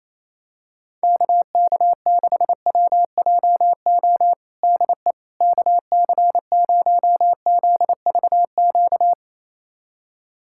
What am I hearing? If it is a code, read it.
Morse code, heard as a keyed tone, about 28 wpm, 700 Hz. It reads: KK6WJO DE KC0ZVQ
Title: USA Callsign DE Callsign 28wpm